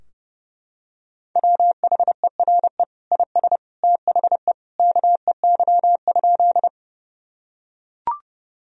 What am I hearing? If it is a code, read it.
Morse code, heard as a keyed tone, about 30 wpm, 700 Hz. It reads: WHERE IS THE KEY?
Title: Sentences from Top 1000 Words - Review Part 4 - 30wpm